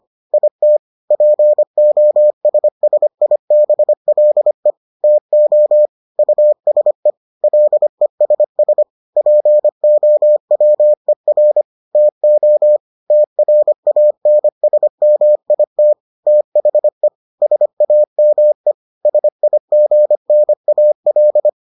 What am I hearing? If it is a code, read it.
Morse code, heard as a keyed tone, about 25 wpm, 600 Hz. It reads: IT POSSIBLE TO USE LESS POWER TO TRANSMIT THE SAME SIGNAL